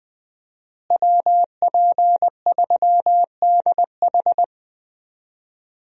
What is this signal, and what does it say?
Morse code, heard as a keyed tone, about 20 wpm, 700 Hz. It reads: WP3DH
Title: USA Call Signs - 5 Characters - Repeated 2x - 20wpm